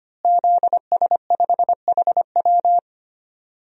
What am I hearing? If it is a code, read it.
Morse code, heard as a keyed tone, about 25 wpm, 700 Hz. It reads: ZS5HW
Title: DXCC-100 Call Signs - 5 Characters - Repeated 3x - 25wpm